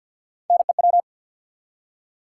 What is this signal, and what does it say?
Morse code, heard as a keyed tone, about 53 wpm, 700 Hz. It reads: NEW
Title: Three Letter Words - 50wpm